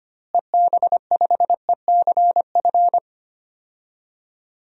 Morse code, heard as a keyed tone, about 25 wpm, 700 Hz. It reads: EB5ECF
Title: DXCC-100 Call Signs - 6 Characters - Repeated 2x - 25wpm